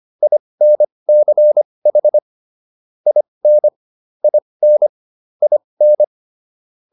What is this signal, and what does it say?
Morse code, heard as a keyed tone, about 25 wpm, spaced as for 20 wpm, 600 Hz. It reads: INCH IN IN IN